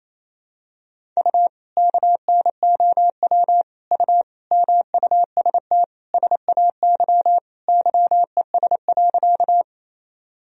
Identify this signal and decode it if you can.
Morse code, heard as a keyed tone, about 28 wpm, 700 Hz. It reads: U KNOW U MUST SAY YES.
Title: Sentences from Top 500 Words and Abbreviations 28wpm